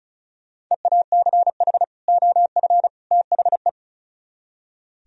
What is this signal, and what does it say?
Morse code, heard as a keyed tone, about 45 wpm, 700 Hz. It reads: EACH OF THE